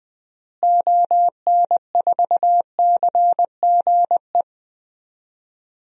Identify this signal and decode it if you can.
Morse code, heard as a keyed tone, about 20 wpm, 700 Hz. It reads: ON4CGE
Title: DXCC-100 Call Signs - 6 Characters - Repeated 2x - 20wpm